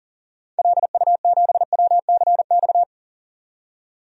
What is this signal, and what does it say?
Morse code, heard as a keyed tone, about 40 wpm, 700 Hz. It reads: LU7WCX
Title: DXCC-100 Call Signs - 6 Characters - 40wpm